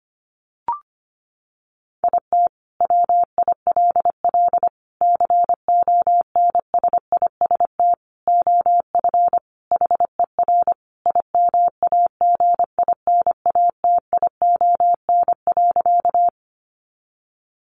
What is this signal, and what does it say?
Morse code, heard as a keyed tone, about 25 wpm, 700 Hz. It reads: IT WILL CONSIST OF HER IMAGINATION.
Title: Sentences from Top 1400 Words 25wpm